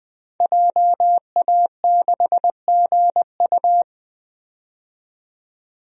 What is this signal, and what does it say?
Morse code, heard as a keyed tone, about 20 wpm, 700 Hz. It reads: JA6GU
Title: DXCC-100 Call Signs - 5 Characters - 20wpm